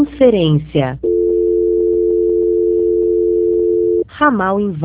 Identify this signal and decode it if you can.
Dial tone (North America), 350 + 440 Hz, with speech.
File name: out- -2059- - - .103